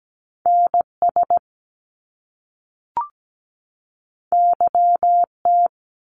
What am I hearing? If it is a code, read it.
Morse code, heard as a keyed tone, about 17 wpm, 700 Hz. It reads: NS YT
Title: Canadian Provinces and Territories 17wpm